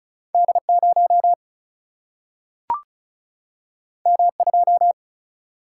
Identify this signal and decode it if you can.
Morse code, heard as a keyed tone, about 35 wpm, 700 Hz. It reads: D0 M2
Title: Single-Letter Single-Number 35wpm